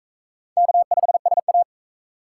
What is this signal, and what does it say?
Morse code, heard as a keyed tone, about 40 wpm, 700 Hz. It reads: K5SA